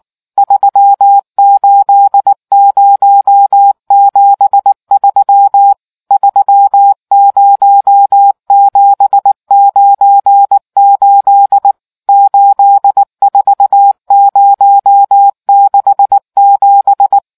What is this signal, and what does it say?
Morse code, heard as a keyed tone, about 19 wpm, 800 Hz. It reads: 38073 30798 84067